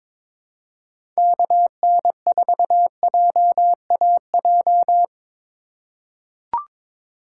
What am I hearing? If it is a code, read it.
Morse code, heard as a keyed tone, about 21 wpm, 700 Hz. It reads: KN4JAJ
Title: USA Call Signs - 6 Characters - 4x Character Spacing - 22wpm